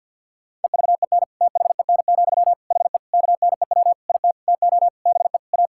Morse code, heard as a keyed tone, about 50 wpm, 700 Hz. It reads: EVEN THEN, HE KNEW IT TO BE A
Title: Sentences from Top 1600 Words 50wpm